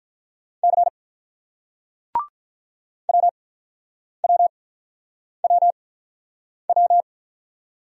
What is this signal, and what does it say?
Morse code, heard as a keyed tone, about 50 wpm, spaced as for 9 wpm, 700 Hz. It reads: C WWWW